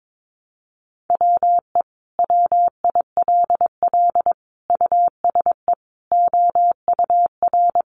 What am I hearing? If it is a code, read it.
Morse code, heard as a keyed tone, about 22 wpm, 700 Hz. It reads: WE WILL USE OUR
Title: Sentences from Top 1100 Words 22wpm